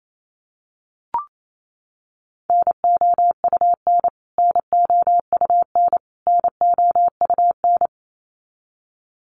Morse code, heard as a keyed tone, about 28 wpm, 700 Hz. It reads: NOUN NOUN NOUN
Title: Top 500 Words Repeated 28wpm